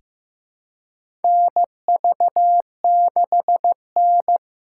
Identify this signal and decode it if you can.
Morse code, heard as a keyed tone, about 15 wpm, 700 Hz. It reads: NV6N
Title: USA Call Signs - 3x Character Spacing - 15wpm